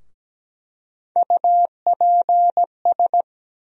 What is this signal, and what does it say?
Morse code, heard as a keyed tone, about 17 wpm, 700 Hz. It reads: UPS
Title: USA Call Sign Postfixes - 3 Letters 17wpm